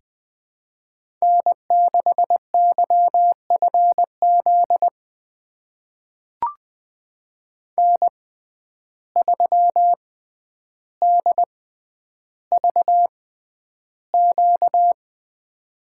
Morse code, heard as a keyed tone, about 20 wpm, spaced as for 10 wpm, 700 Hz. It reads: N6YFZ N3DVQ
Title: USA Call Signs - 5 Characters - 6x Character Spacing - 20wpm